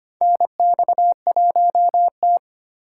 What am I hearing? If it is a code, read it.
Morse code, heard as a keyed tone, about 25 wpm, 700 Hz. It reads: NX1T
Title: USA Call Signs - 4 Characters - 8x Character Spacing - 25wpm